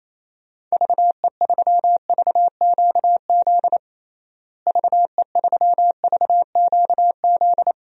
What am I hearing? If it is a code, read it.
Morse code, heard as a keyed tone, about 28 wpm, 700 Hz. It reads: VE3VQZ VE3VQZ
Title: DXCC-100 Call Signs - 6 Characters - Repeated 2x - 28wpm